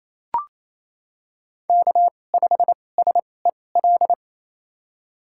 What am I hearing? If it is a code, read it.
Morse code, heard as a keyed tone, about 28 wpm, spaced as for 20 wpm, 700 Hz. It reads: K5SEL
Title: USA Call Signs - 2x Character Spacing - 28wpm